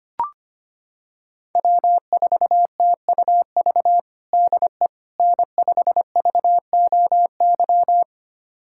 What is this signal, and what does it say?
Morse code, heard as a keyed tone, about 25 wpm, 700 Hz. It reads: W4TUV DE N5VOY
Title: USA Callsign DE Callsign - 5 Characters - 25wpm